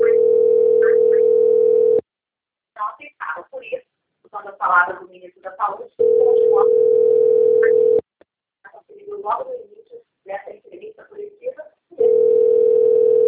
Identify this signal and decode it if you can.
Ringback tone (North America), 440 + 480 Hz, with speech.
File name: rg-1600-unknown- - - .32533